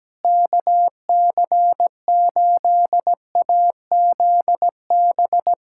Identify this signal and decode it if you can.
Morse code, heard as a keyed tone, about 17 wpm, 700 Hz. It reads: KC8AZB
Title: K1USN SST QSO Practice 17wpm